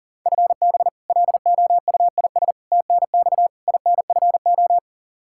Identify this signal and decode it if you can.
Morse code, heard as a keyed tone, about 37 wpm, 700 Hz. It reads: FB LOUIS TNX INFO
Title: Practice QSO - Exchange 3 - More Info - 40wpm